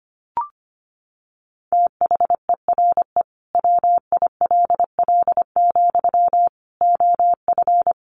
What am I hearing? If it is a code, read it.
Morse code, heard as a keyed tone, about 25 wpm, 700 Hz. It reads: THERE WILL, OF
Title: Sentences from Top 1400 Words 25wpm